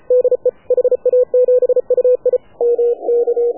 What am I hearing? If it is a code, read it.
Morse code, heard as a keyed tone, about 33 wpm, 500 Hz. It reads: DE HA7UI MK
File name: AUDIO2019 / HA7UI CW